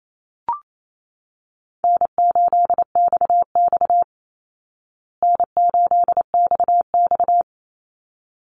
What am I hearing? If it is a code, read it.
Morse code, heard as a keyed tone, about 28 wpm, 700 Hz. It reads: N8XX N8XX
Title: Easy Call Signs Repeated 2x - 28wpm